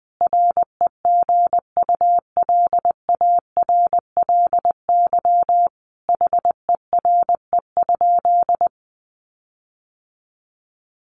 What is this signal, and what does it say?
Morse code, heard as a keyed tone, about 20 wpm, 700 Hz. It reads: REGULARLY HERE?
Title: Sentences from Top 1400 Words 20wpm